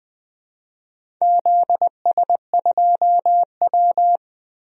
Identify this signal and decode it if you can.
Morse code, heard as a keyed tone, about 21 wpm, 700 Hz. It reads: ZS2W
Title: DXCC-100 Call Signs - 4 Characters - Repeated 2x - 20wpm